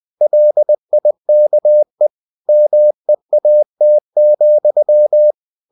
Morse code, heard as a keyed tone, about 20 wpm, 600 Hz. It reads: LIKE MEAT,
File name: the old man and the sea part 2 20wpm